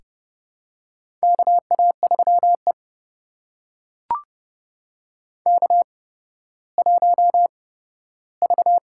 Morse code, heard as a keyed tone, about 30 wpm, spaced as for 15 wpm, 700 Hz. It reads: KA3E K1V
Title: USA Call Signs - 4 Characters - 8x Character Spacing - 30wpm